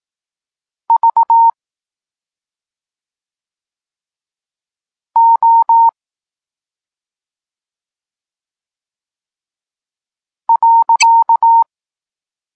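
Morse code, heard as a keyed tone, about 18 wpm, 920 Hz. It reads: V O .
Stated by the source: Morse code lesson for learning the letters K and X, from the SP9MOA Niepołomicki Klub Krótkofalowców course.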